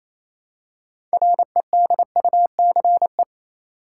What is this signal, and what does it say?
Morse code, heard as a keyed tone, about 28 wpm, 700 Hz. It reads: REDUCE
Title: Six Letter Words - 28wpm